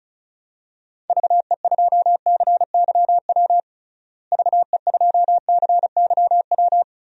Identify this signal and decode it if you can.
Morse code, heard as a keyed tone, about 36 wpm, 700 Hz. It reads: VE2CYW VE2CYW
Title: DXCC-100 Call Signs - 6 Characters - Repeated 3x - 35wpm